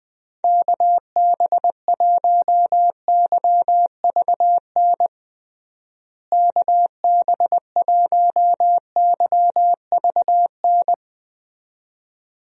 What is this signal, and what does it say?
Morse code, heard as a keyed tone, about 20 wpm, 700 Hz. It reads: KB1YVN KB1YVN
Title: USA Call Signs - 6 Characters - Repeated 2x - 20wpm